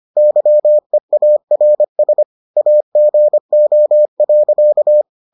Morse code, heard as a keyed tone, about 25 wpm, 600 Hz. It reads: YEARS AGO.